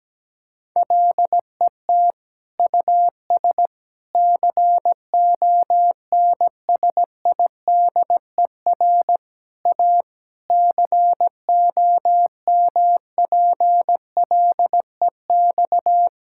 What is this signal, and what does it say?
Morse code, heard as a keyed tone, about 17 wpm, 700 Hz. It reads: LET US CONSIDER A COMPLEX
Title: Sentences from Top 1000 Words - Review Part 6 - 17wpm